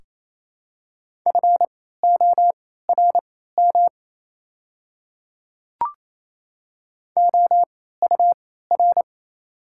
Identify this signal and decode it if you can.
Morse code, heard as a keyed tone, about 28 wpm, spaced as for 16 wpm, 700 Hz. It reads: FORM OUR
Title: 3x Character Spacing - Top 200 Words - 28wpm